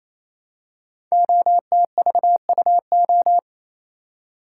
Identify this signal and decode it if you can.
Morse code, heard as a keyed tone, about 28 wpm, 700 Hz. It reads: OTVUO
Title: Code Groups - 5 Letter:Numbers - 28wpm